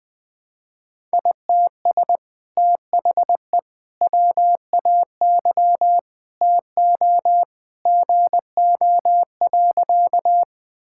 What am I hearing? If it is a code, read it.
Morse code, heard as a keyed tone, about 21 wpm, 700 Hz. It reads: ITS THE WAY TO GO.
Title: Sentences with 2-Word Phrases - Spoken Beginning - 20wpm